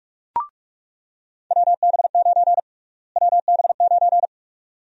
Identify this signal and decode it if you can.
Morse code, heard as a keyed tone, about 41 wpm, 700 Hz. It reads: WB9 WB9
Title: USA Call Sign Prefixes - Repeated 2x - 45wpm